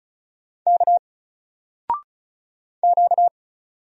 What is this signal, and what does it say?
Morse code, heard as a keyed tone, about 35 wpm, 700 Hz. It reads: K Q
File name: Single-Letter-Number-35wpm